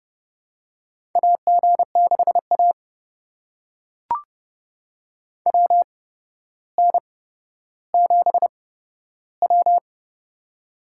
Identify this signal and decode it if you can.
Morse code, heard as a keyed tone, about 30 wpm, spaced as for 13 wpm, 700 Hz. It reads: AG6A WN7W
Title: USA Call Signs - 4 Characters - 8x Character Spacing - 30wpm